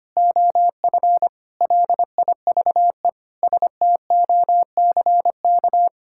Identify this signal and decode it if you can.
Morse code, heard as a keyed tone, about 25 wpm, 700 Hz. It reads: OF LIVE STOCK
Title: Sentences from Top 1000 Words - Review Part 5 - 25wpm